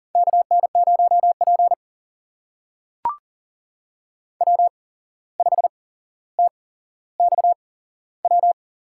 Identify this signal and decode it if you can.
Morse code, heard as a keyed tone, about 40 wpm, spaced as for 16 wpm, 700 Hz. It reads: KN0P W5TXW
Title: USA Call Signs - 8x Character Spacing - 40wpm